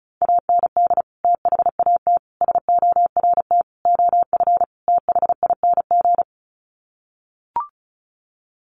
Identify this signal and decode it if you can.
Morse code, heard as a keyed tone, about 34 wpm, 700 Hz. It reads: AND THAT SORT OF THING
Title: Sets of 5 Words Encore2 35wpm